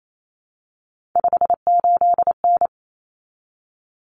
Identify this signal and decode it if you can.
Morse code, heard as a keyed tone, about 28 wpm, 700 Hz. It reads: 58N
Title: Common Cut Numbers - 3-Digit 28wpm